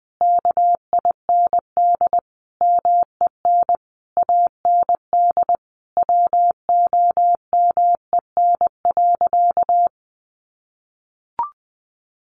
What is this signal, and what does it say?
Morse code, heard as a keyed tone, about 20 wpm, 700 Hz. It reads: KIND MEN AND WOMEN.
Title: Sentences from Top 1300 Words 20wpm